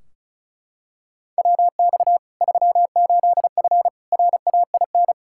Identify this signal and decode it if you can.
Morse code, heard as a keyed tone, about 36 wpm, 700 Hz. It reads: WX 38F RAIN
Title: Practice QSO - Exchange 2 - Info - 35wpm